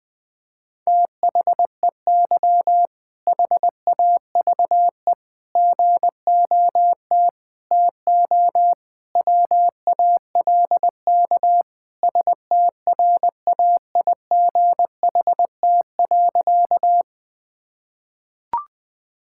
Morse code, heard as a keyed tone, about 21 wpm, 700 Hz. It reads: THEY HAVE GOT TO WALK STRAIGHT.
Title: Sentences from Top 1000 Words - Review Part 6 - 20wpm